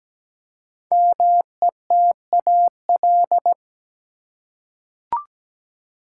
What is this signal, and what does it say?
Morse code, heard as a keyed tone, about 17 wpm, 700 Hz. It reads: METAL
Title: Five Letter Words - 17wpm